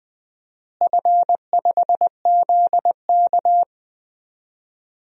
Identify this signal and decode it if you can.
Morse code, heard as a keyed tone, about 20 wpm, 700 Hz. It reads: F5ZK
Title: DXCC-100 Call Signs - 4 Characters - Repeated 2x - 20wpm